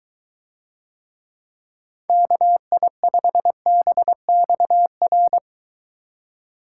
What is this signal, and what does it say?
Morse code, heard as a keyed tone, about 23 wpm, 700 Hz. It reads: KI5BXR